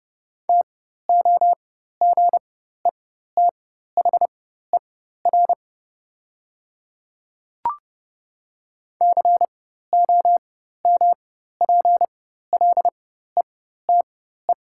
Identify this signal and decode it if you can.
Morse code, heard as a keyed tone, about 30 wpm, spaced as for 14 wpm, 700 Hz. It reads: TOGETHER COMPLETE
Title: 4x Character Spacing - Top 500 Words - 30wpm